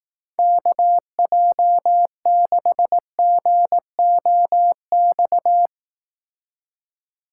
Morse code, heard as a keyed tone, about 18 wpm, 700 Hz. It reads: KJ6GOX
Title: Speed Racing - USA 2x3 Call Signs 15wpm